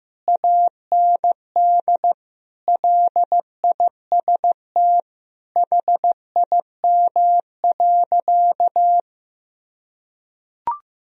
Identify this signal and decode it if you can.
Morse code, heard as a keyed tone, about 16 wpm, 700 Hz. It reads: AND LIST HIM.
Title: Sentences from Top 1000 Words - Review Part 2 - 15wpm